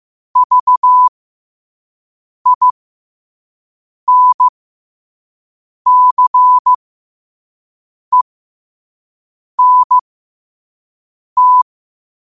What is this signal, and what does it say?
Morse code, heard as a keyed tone, about 15 wpm, spaced as for 5 wpm, 1000 Hz. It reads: VINCENT
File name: qotdfiles / 5wpm